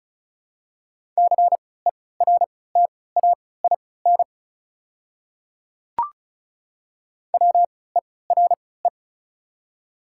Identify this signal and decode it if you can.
Morse code, heard as a keyed tone, about 36 wpm, spaced as for 20 wpm, 690 Hz. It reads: CERTAIN WERE
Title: 3x Character Spacing - Top 500 Words - 35wpm